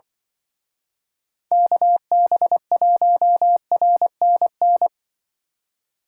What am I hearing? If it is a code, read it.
Morse code, heard as a keyed tone, about 25 wpm, 700 Hz. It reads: KB1RNN
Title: Speed Racing - USA Call Signs 20wpm